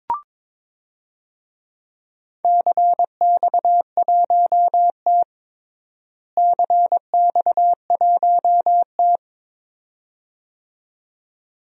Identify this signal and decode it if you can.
Morse code, heard as a keyed tone, about 22 wpm, 700 Hz. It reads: CX1T CX1T
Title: DXCC-100 Call Signs - 4 Characters - Repeated 2x - 22wpm